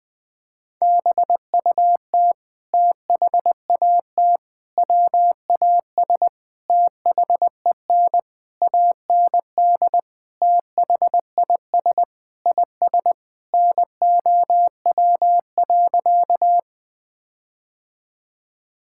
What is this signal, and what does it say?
Morse code, heard as a keyed tone, about 20 wpm, 700 Hz. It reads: BUT THAT WAS THEN AND THIS IS NOW.
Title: Sentences with 8-Word Phrases - 20wpm